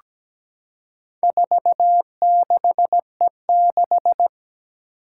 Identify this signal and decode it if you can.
Morse code, heard as a keyed tone, about 17 wpm, 700 Hz. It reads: 46E6
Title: Four Challenging Letter-Number Combinations 17wpm